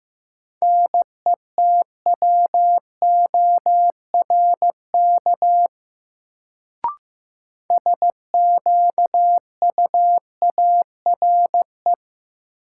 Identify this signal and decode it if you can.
Morse code, heard as a keyed tone, about 15 wpm, 700 Hz. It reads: NETWORK SQUARE
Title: Sentences from Top 1000 Words - Review Part 7 - New Word Drill - 15wpm